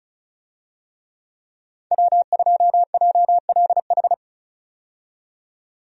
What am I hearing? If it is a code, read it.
Morse code, heard as a keyed tone, about 36 wpm, 700 Hz. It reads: W2JLH
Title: USA Call Signs - 5 Characters - 35wpm